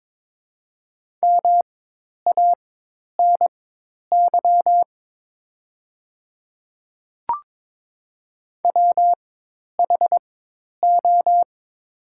Morse code, heard as a keyed tone, about 22 wpm, spaced as for 10 wpm, 700 Hz. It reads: MANY WHO
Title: 4x Character Spacing - Top 200 Words - 22wpm